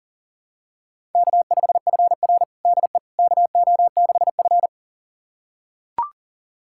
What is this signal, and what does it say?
Morse code, heard as a keyed tone, about 40 wpm, 700 Hz. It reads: K5FR DE KO6F
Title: USA Callsign DE Callsign - 4 Characters - 40wpm